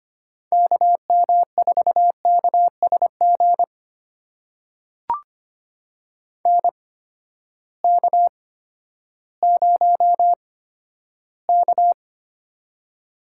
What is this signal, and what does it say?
Morse code, heard as a keyed tone, about 25 wpm, spaced as for 12 wpm, 700 Hz. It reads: KM4KSG NK0K
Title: USA Call Signs - 8x Character Spacing - 25wpm